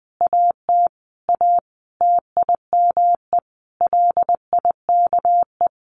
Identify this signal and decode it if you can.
Morse code, heard as a keyed tone, about 20 wpm, 700 Hz. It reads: AT A TIME LIKE
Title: Sentences from Top 600 Words 20wpm